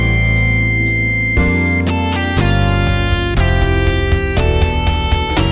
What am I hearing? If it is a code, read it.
Dial-up modem handshake.